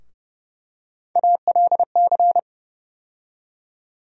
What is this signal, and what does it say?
Morse code, heard as a keyed tone, about 30 wpm, 700 Hz. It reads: ALC
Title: USA Call Sign Postfixes - 3 Letters 30wpm